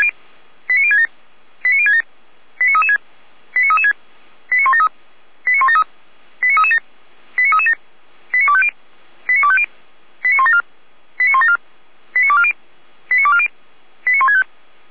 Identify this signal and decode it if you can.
Five-tone selective call (ZVEI), heard as quick runs of five tones, tones of about 70 ms. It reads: 89067 89067 89336 89336 89173 89173 89337 89337 89250 89250 89174 89174 89240 89240 89165
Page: /455/SDR/ZVEI/89108/